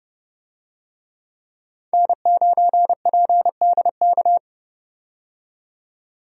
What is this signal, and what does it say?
Morse code, heard as a keyed tone, about 31 wpm, 700 Hz. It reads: N9PDK
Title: USA Call Signs - 5 Characters - 30wpm